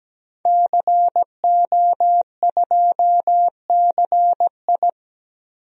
Morse code, heard as a keyed tone, about 17 wpm, 700 Hz. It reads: CO2CI
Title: DXCC-100 Call Signs - 5 Characters - 17wpm